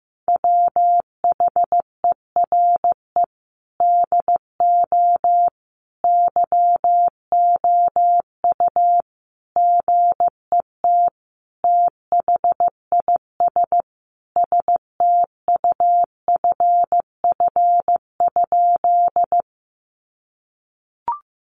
Morse code, heard as a keyed tone, about 15 wpm, 700 Hz. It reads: WHERE DO YOU GET THIS STUFF?
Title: Sentences from Top 1400 Words 15wpm